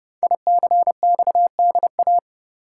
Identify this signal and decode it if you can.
Morse code, heard as a keyed tone, about 28 wpm, 700 Hz. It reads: ICXDA
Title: Code Groups - 5 Letter:Numbers - 30wpm